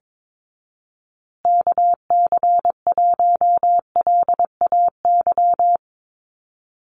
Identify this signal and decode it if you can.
Morse code, heard as a keyed tone, about 22 wpm, 700 Hz. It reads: KC1LAY
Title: USA Call Signs Repeated 2x 22wpm